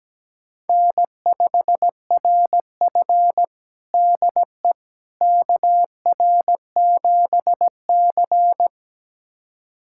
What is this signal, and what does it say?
Morse code, heard as a keyed tone, about 17 wpm, 700 Hz. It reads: N5RF DE KR7C
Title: USA Callsign DE Callsign - 4 Characters - 17wpm